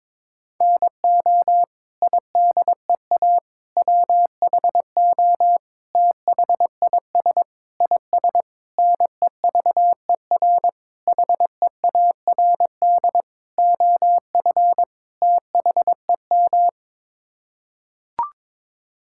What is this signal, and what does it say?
Morse code, heard as a keyed tone, about 22 wpm, 700 Hz. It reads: NO IDEA WHO THIS IS NEVER HEARD OF THEM
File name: Sets of 9 Words - 22wpm